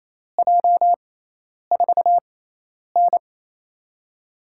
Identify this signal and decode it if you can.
Morse code, heard as a keyed tone, about 28 wpm, spaced as for 10 wpm, 700 Hz. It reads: J4N
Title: USA Call Signs - 6x Character Spacing - 28wpm